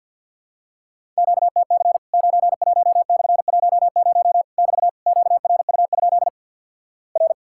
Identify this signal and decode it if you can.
Morse code, heard as a keyed tone, about 53 wpm, 700 Hz. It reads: QTC 91/10 <BT> QRU? R
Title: Worked All Europe DX Contest - QTC Exchange - 50wpm